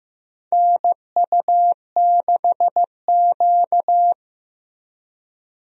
Morse code, heard as a keyed tone, about 15 wpm, 700 Hz. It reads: NU6Q
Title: USA Call Signs - 4 Characters - Repeated 3x - 15wpm